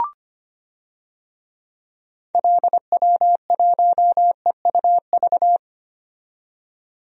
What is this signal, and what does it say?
Morse code, heard as a keyed tone, about 25 wpm, 700 Hz. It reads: LW1EUV
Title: DXCC-100 Call Signs - 6 Characters - 25wpm